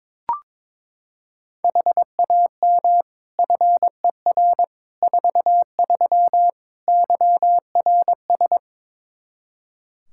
Morse code, heard as a keyed tone, about 21 wpm, 700 Hz. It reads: HAM FER 43 YRS
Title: Practice QSO - Exchange 3 - More Info - 22wpm